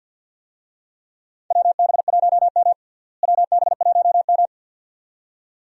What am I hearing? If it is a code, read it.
Morse code, heard as a keyed tone, about 52 wpm, 700 Hz. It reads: WB1M WB1M
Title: USA Call Signs - 4 Characters - Repeated 2x - 50wpm